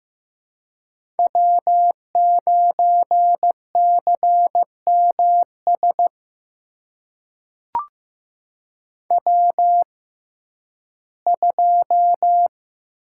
Morse code, heard as a keyed tone, about 15 wpm, spaced as for 11 wpm, 700 Hz. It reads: W9CMS W2
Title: USA Call Signs - 5 Characters - 6x Character Spacing - 15wpm